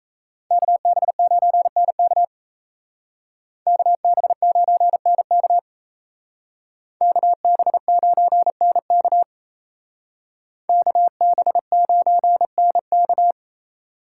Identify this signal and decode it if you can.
Morse code, heard as a keyed tone, about 34 wpm, 700 Hz. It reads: KB9NK KB9NK KB9NK KB9NK
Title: Speed Racing - USA 2x2 Call Signs 28wpm